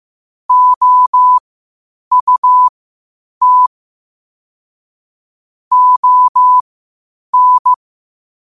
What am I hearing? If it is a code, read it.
Morse code, heard as a keyed tone, about 15 wpm, spaced as for 8 wpm, 1000 Hz. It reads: OUT ON
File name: qotdfiles / 8wpm